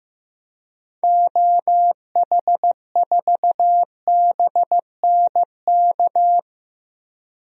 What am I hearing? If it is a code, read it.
Morse code, heard as a keyed tone, about 15 wpm, 690 Hz. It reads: OH4BNK
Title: DXCC-100 Call Signs - 6 Characters - 15wpm